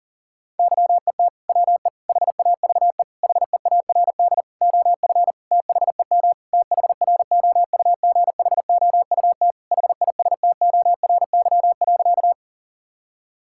Morse code, heard as a keyed tone, about 40 wpm, 700 Hz. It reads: YET WE HAVE HEARD OF THEM THROUGHOUT HISTORY.
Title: Sentences from Top 1000 Words - Review Part 5 - 40wpm